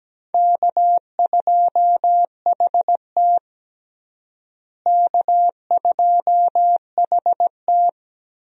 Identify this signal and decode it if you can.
Morse code, heard as a keyed tone, about 17 wpm, 700 Hz. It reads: K2HT K2HT
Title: USA Call Signs - 4 Characters - Repeated 3x - 17wpm